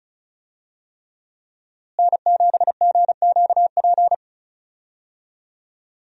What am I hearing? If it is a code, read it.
Morse code, heard as a keyed tone, about 34 wpm, 700 Hz. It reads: N7GQP
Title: USA Call Signs - 5 Characters - Repeated 3x - 35wpm